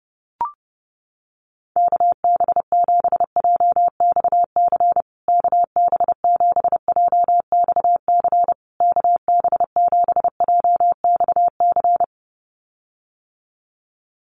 Morse code, heard as a keyed tone, about 30 wpm, 700 Hz. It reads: KB7JXC KB7JXC KB7JXC
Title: USA Call Signs Repeated 3x 30wpm